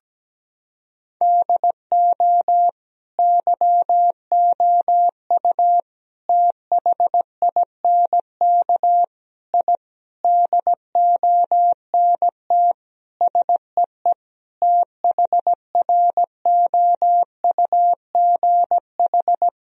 Morse code, heard as a keyed tone, about 17 wpm, 700 Hz. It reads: DO YOU THINK I DONT SEE THROUGH
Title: Sentences from Top 1000 Words - Review Part 1 - 17wpm